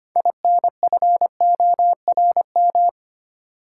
Morse code, heard as a keyed tone, about 25 wpm, 700 Hz. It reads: INFORM
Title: Six Letter Words - 25wpm